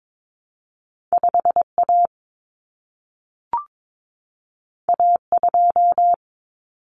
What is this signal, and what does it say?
Morse code, heard as a keyed tone, about 22 wpm, 700 Hz. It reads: 5A A2
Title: Common Cut Numbers - 2-Digit 22wpm